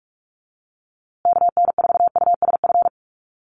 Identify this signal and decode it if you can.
Morse code, heard as a keyed tone, about 44 wpm, 700 Hz. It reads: KN4USF
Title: USA Call Signs Repeated 2x 45wpm